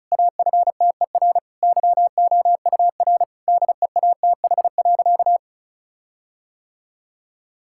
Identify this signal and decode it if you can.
Morse code, heard as a keyed tone, about 36 wpm, 700 Hz. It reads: AFTER YOUR DEATH.
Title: Sentences from Top 1600 Words 35wpm